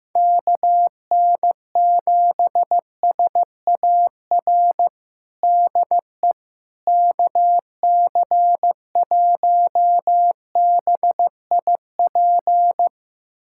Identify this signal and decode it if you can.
Morse code, heard as a keyed tone, about 15 wpm, 700 Hz. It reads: KN7SAR DE KC1BIP
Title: USA Callsign DE Callsign 15wpm